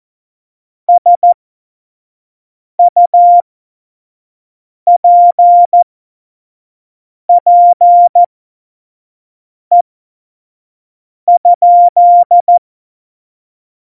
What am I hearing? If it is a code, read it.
Morse code, heard as a keyed tone, about 14 wpm, spaced as for 5 wpm, 700 Hz. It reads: SUPPE?
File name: Morsekurs / Witze / Witz 02. Juli 2019, 25 F70